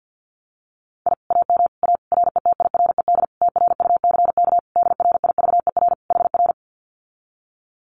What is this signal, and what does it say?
Morse code, heard as a keyed tone, about 50 wpm, 700 Hz. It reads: I AM A RETIRED TRUCK DRIVER HR
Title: Common QSO Phrases Encore 50wpm